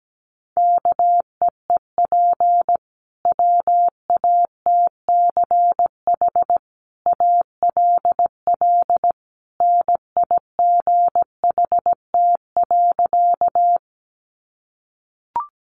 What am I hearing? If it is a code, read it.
Morse code, heard as a keyed tone, about 17 wpm, 700 Hz. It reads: KEEP WATCH ALL NIGHT.
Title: Sentences from Top 300 Words 17wpm